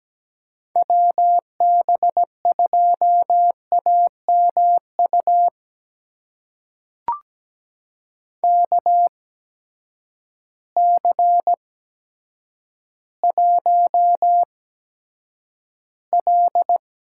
Morse code, heard as a keyed tone, about 17 wpm, spaced as for 9 wpm, 700 Hz. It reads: WB2AMU KC1L
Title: USA Call Signs - 6 Characters - 8x Character Spacing - 17wpm